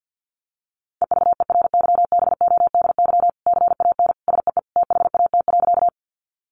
Morse code, heard as a keyed tone, about 50 wpm, 700 Hz. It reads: EVERYBODY CAN SEE THAT.
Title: Sentences from Top 1400 Words 50wpm